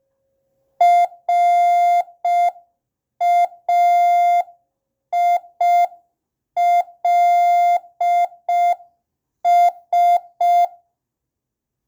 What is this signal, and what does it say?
Morse code, heard as a keyed tone, about 5 wpm, 700 Hz. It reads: RAILS